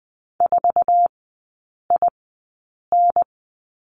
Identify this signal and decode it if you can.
Morse code, heard as a keyed tone, about 20 wpm, spaced as for 8 wpm, 700 Hz. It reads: 4IN
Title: 047 Lesson 9e - Number 4 - Three Characters 20wpm